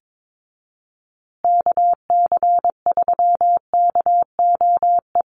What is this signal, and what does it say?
Morse code, heard as a keyed tone, about 22 wpm, 700 Hz. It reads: KC3KOE
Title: USA Call Signs Repeated 3x 22wpm